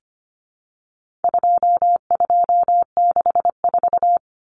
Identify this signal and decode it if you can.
Morse code, heard as a keyed tone, about 25 wpm, 700 Hz. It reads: 2264
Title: Numbers - 4-Digit 25wpm